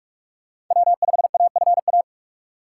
Morse code, heard as a keyed tone, about 44 wpm, 700 Hz. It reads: W5AFA